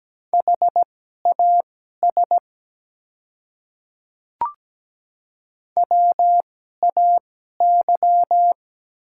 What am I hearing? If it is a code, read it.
Morse code, heard as a keyed tone, about 17 wpm, spaced as for 12 wpm, 700 Hz. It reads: HAS WAY
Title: 2x Character Spacing - Top 200 Words - 17wpm